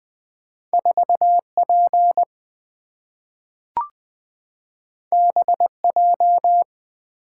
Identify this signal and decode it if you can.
Morse code, heard as a keyed tone, about 20 wpm, 700 Hz. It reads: 4 P B J
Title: Two Letter-Number Combinations - 20wpm